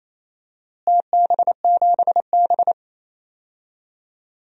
Morse code, heard as a keyed tone, about 28 wpm, 700 Hz. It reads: TB7B
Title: DXCC-100 Call Signs - 4 Characters - Repeated 2x - 28wpm